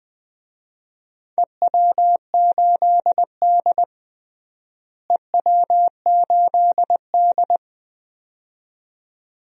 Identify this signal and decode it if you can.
Morse code, heard as a keyed tone, about 21 wpm, 700 Hz. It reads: EW8D EW8D
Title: DXCC-100 Call Signs - 4 Characters - Repeated 2x - 20wpm